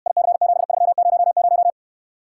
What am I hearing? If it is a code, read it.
Morse code, heard as a keyed tone, about 69 wpm, 700 Hz. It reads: IC7300